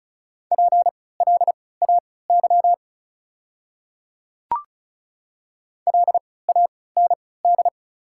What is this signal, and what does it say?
Morse code, heard as a keyed tone, about 34 wpm, spaced as for 20 wpm, 700 Hz. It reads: PLAY LAND
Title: 3x Character Spacing - Top 300 Words - 35wpm